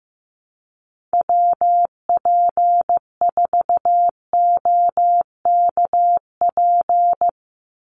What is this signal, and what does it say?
Morse code, heard as a keyed tone, about 15 wpm, 700 Hz. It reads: WP4OKP
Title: USA Call Signs 15wpm